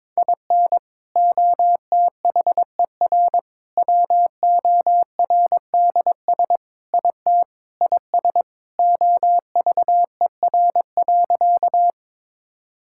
Morse code, heard as a keyed tone, about 22 wpm, 700 Hz. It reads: IN OTHER WORDS IT IS OVER.
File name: practice / 22wpm / Sentences with 3-Word Phrases - 22wpm